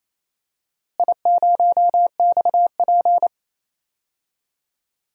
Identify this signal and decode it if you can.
Morse code, heard as a keyed tone, about 28 wpm, 700 Hz. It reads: I0XP
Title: DXCC-100 Call Signs - 4 Characters - Repeated 3x - 28wpm